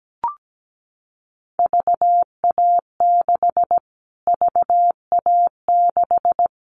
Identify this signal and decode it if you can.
Morse code, heard as a keyed tone, about 17 wpm, 700 Hz. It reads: VA6 VA6
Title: Callsign Prefixes for Top 25 DXCC - Repeated 3x - 17wpm